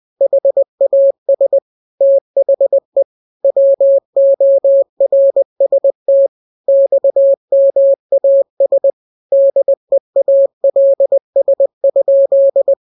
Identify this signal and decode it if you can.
Morse code, heard as a keyed tone, about 20 wpm, 550 Hz. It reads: HAS THE WORST XMAS DEALS?